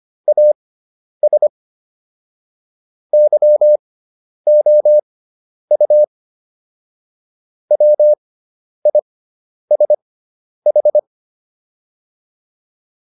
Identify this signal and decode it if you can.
Morse code, heard as a keyed tone, about 25 wpm, spaced as for 10 wpm, 600 Hz. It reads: AS YOU WISH